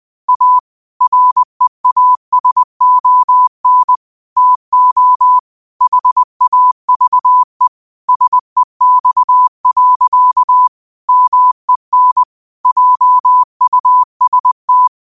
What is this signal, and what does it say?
Morse code, heard as a keyed tone, about 20 wpm, 1000 Hz. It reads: A REASON TO HAVE SEX. MEN JUST
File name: 20wpm